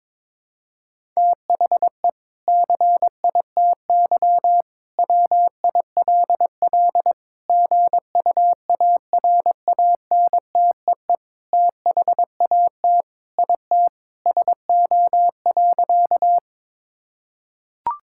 Morse code, heard as a keyed tone, about 22 wpm, 700 Hz. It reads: THE CITY WILL GUARANTEE THAT IT SO.
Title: Sentences from Top 1600 Words 22wpm